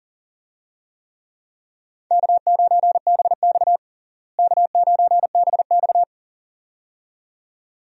Morse code, heard as a keyed tone, about 40 wpm, 700 Hz. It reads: K9BX K9BX
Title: USA Call Signs - 4 Characters - Repeated 2x - 40wpm